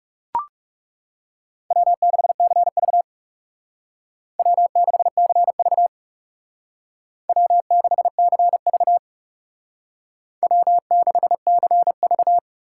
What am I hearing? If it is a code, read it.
Morse code, heard as a keyed tone, about 39 wpm, 700 Hz. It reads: W6CV W6CV W6CV W6CV